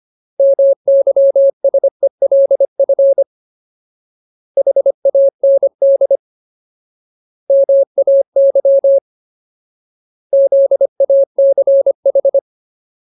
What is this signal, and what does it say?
Morse code, heard as a keyed tone, about 25 wpm, 550 Hz. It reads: MYSELF HAND MAY ZACH
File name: Ultimate Word List / 25WPM Ultimate